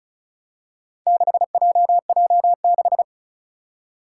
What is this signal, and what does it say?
Morse code, heard as a keyed tone, about 36 wpm, 700 Hz. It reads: 6JJ6
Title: Four Challenging Letter-Number Combinations 35wpm